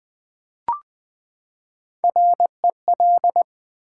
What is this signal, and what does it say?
Morse code, heard as a keyed tone, about 20 wpm, 700 Hz. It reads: REL
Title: Build A Word - 3 Letters At A Time - 8-Letter Words - 20wpm